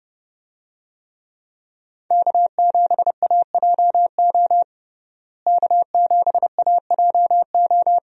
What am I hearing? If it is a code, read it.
Morse code, heard as a keyed tone, about 30 wpm, 700 Hz. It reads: K7AJO K7AJO
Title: USA Call Signs - 5 Characters - Repeated 3x - 30wpm